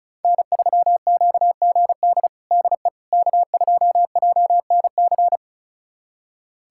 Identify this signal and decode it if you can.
Morse code, heard as a keyed tone, about 36 wpm, 700 Hz. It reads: N3QGD DE K2JNC
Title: USA Callsign DE Callsign - 5 Characters - 35wpm